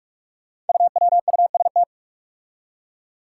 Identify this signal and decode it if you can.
Morse code, heard as a keyed tone, about 44 wpm, 700 Hz. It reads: UWUST
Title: Five Challenging Letter-Number Combinations 45wpm